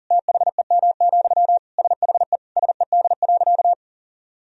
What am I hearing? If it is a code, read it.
Morse code, heard as a keyed tone, about 40 wpm, 700 Hz. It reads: THEM, SHE SED.
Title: Sentences from Top 400 Words and Abbreviations 40wpm